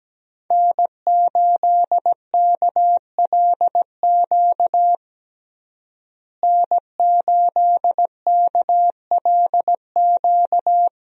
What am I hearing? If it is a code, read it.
Morse code, heard as a keyed tone, about 17 wpm, 700 Hz. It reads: N8KLQ N8KLQ
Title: USA Call Signs - 5 Characters - Repeated 3x - 17wpm